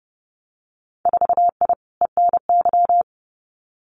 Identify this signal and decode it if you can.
Morse code, heard as a keyed tone, about 30 wpm, 700 Hz. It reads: 4I ENY
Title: Winter Field Day Exchanges Encore 30wpm